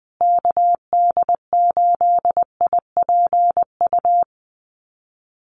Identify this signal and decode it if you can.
Morse code, heard as a keyed tone, about 20 wpm, 700 Hz. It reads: KD8IPU
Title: USA Call Signs Repeated 2x 20wpm